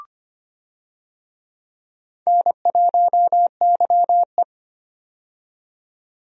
Morse code, heard as a keyed tone, about 25 wpm, 700 Hz. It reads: N1YE
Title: USA Call Signs - 4 Characters - 25wpm